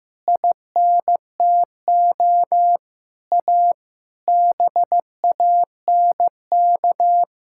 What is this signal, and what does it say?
Morse code, heard as a keyed tone, about 16 wpm, 700 Hz. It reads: INTO A BANK
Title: Sentences from Top 1000 Words - Review Part 6 - 15wpm